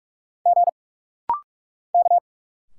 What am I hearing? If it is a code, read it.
Morse code, heard as a keyed tone, about 46 wpm, 700 Hz. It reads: G K
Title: Single Letters - Rapid-Fire 45wpm